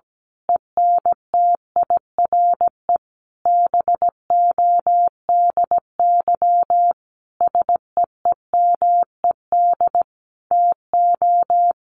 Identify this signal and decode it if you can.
Morse code, heard as a keyed tone, about 17 wpm, 700 Hz. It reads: ENTIRE BODY SEEMED TO
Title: A Sound Of Thunder 17wpm